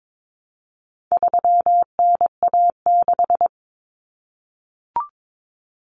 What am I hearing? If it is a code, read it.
Morse code, heard as a keyed tone, about 22 wpm, 700 Hz. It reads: 3NA6
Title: Common Cut Numbers - 4-Digit 22wpm